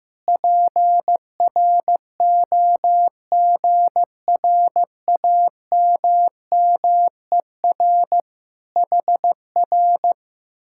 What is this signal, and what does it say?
Morse code, heard as a keyed tone, about 15 wpm, 700 Hz. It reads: PROGRAMMER HR
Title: Speed-Racing Common QSO Phrases Encore - 15wpm